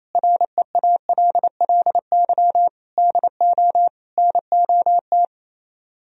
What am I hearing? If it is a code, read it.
Morse code, heard as a keyed tone, about 32 wpm, 700 Hz. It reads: REALLY DO NOT
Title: Speed-Racing - Sets of 4 Words 28wpm